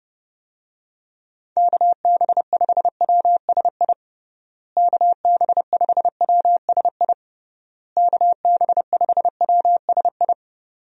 Morse code, heard as a keyed tone, about 30 wpm, 700 Hz. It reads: KB5WSI KB5WSI KB5WSI
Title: USA Call Signs - 6 Characters - Repeated 3x - 30wpm